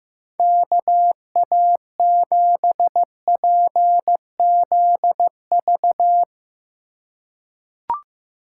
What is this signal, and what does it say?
Morse code, heard as a keyed tone, about 15 wpm, 700 Hz. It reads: KA7PZV
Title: USA Call Signs - 6 Characters - 2x Character Spacing - 15wpm